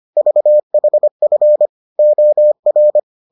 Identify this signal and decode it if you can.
Morse code, heard as a keyed tone, about 25 wpm, 600 Hz. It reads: VHF OR